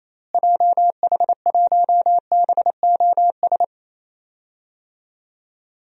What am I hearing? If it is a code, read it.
Morse code, heard as a keyed tone, about 28 wpm, 700 Hz. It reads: JH1BOS
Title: DXCC-100 Call Signs - 6 Characters - Repeated 2x - 28wpm